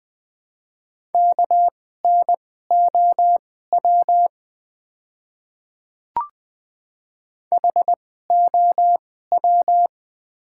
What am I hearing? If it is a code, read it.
Morse code, heard as a keyed tone, about 20 wpm, spaced as for 15 wpm, 700 Hz. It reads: KNOW HOW
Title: 2x Character Spacing - Top 100 Words - 20wpm